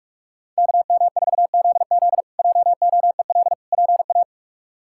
Morse code, heard as a keyed tone, about 41 wpm, 700 Hz. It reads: KM4ZZ JOEL PA
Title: K1USN SST QSO Practice - 45wpm